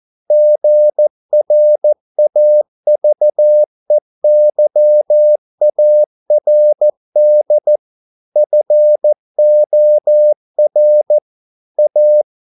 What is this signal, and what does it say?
Morse code, heard as a keyed tone, about 14 wpm, 600 Hz. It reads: GRAVEYARD FOR A
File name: mp3 files / a dogs mistake - 14wpm